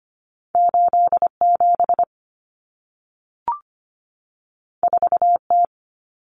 Morse code, heard as a keyed tone, about 25 wpm, 700 Hz. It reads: 87 4T
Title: Common Cut Numbers - 2-Digit 25wpm